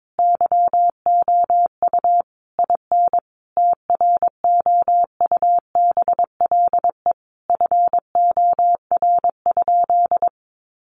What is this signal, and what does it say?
Morse code, heard as a keyed tone, about 22 wpm, 700 Hz. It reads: YOU IN TROUBLE FOR?
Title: Sentences from Top 1200 Words 22wpm